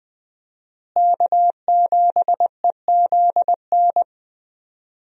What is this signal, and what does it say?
Morse code, heard as a keyed tone, about 20 wpm, 700 Hz. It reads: K7EZN
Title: USA Call Signs - 5 Characters - Repeated 2x - 20wpm